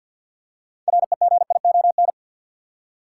Morse code, heard as a keyed tone, about 52 wpm, 700 Hz. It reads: REGION
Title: Six Letter Words - 50wpm